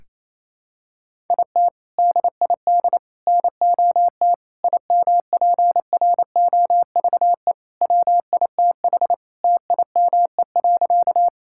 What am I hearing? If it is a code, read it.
Morse code, heard as a keyed tone, about 28 wpm, 700 Hz. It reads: IT DID NOT IMPROVE WITH TIME.
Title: Sentences from Top 1000 Words - Review Part 5 - 28wpm